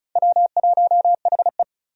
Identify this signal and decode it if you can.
Morse code, heard as a keyed tone, about 36 wpm, 700 Hz. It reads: W1HE
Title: USA Call Signs - 4 Characters - Repeated 3x - 35wpm